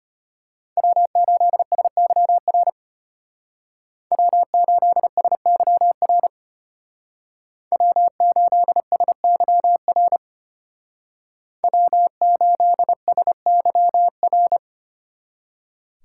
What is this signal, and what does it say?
Morse code, heard as a keyed tone, about 30 wpm, 700 Hz. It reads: W8SYR W8SYR W8SYR W8SYR
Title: Speed Racing - USA 1x3 Call Signs 25wpm